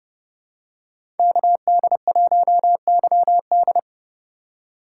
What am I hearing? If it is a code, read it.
Morse code, heard as a keyed tone, about 30 wpm, 700 Hz. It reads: KD1YD
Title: USA Call Signs - 5 Characters - 3x Character Spacing - 30wpm